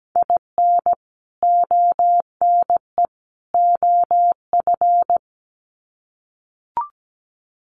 Morse code, heard as a keyed tone, about 17 wpm, 700 Hz. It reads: IN ONE OF
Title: Sets of 3 Words 17wpm